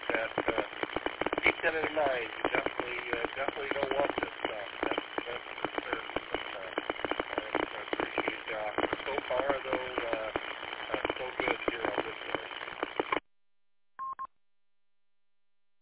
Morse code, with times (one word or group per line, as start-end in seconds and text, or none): none